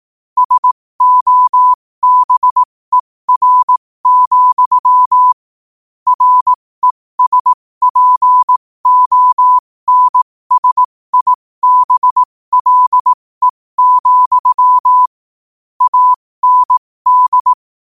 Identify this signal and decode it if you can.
Morse code, heard as a keyed tone, about 18 wpm, spaced as for 15 wpm, 1000 Hz. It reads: SOBER, RESPONSIBLE, AND
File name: qotdfiles / 15wpm